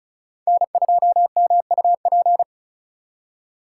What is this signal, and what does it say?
Morse code, heard as a keyed tone, about 46 wpm, 700 Hz. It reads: N2MUP